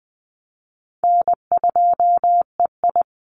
Morse code, heard as a keyed tone, about 19 wpm, 700 Hz. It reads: N2EI
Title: Easy Call Signs Repeated 3x - 20wpm